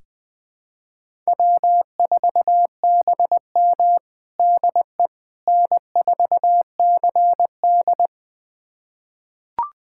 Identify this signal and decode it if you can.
Morse code, heard as a keyed tone, about 21 wpm, 700 Hz. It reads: W4BM DE N4CD
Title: USA Callsign DE Callsign - 4 Characters - 20wpm